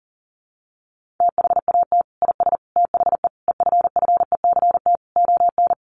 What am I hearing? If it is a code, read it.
Morse code, heard as a keyed tone, about 40 wpm, 700 Hz. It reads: THAT IS THE EFFECT ON
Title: Sentences from Top 1300 Words 40wpm